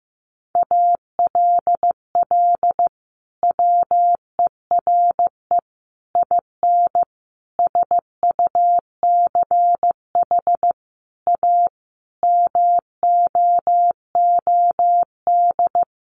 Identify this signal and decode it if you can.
Morse code, heard as a keyed tone, about 15 wpm, 700 Hz. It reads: ALL WERE IN SUCH A MOOD
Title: Sentences from Top 1100 Words 15wpm